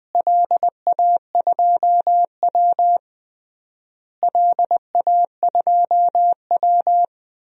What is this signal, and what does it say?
Morse code, heard as a keyed tone, about 20 wpm, 690 Hz. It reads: LA2W LA2W
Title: DXCC-100 Call Signs - 4 Characters - Repeated 3x - 20wpm